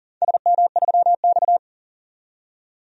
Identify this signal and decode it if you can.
Morse code, heard as a keyed tone, about 40 wpm, 700 Hz. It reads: SM3X
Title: DXCC-100 Call Signs - 4 Characters - 40wpm